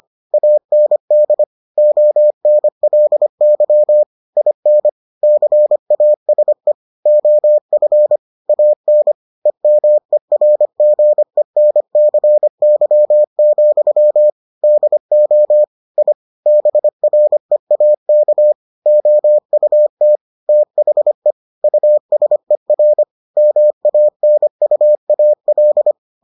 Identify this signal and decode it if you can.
Morse code, heard as a keyed tone, about 25 wpm, 600 Hz. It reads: AND ONLY IN CASE OF AN EMERGENCY, DO I BREAK OUT THE USER MANUAL